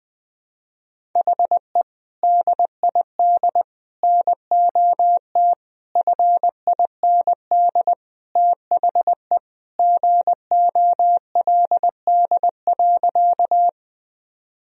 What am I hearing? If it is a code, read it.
Morse code, heard as a keyed tone, about 21 wpm, 700 Hz. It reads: HE DID NOT FIND THE GOLD.
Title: Sentences from Top 1000 Words - Review Part 3 - 20wpm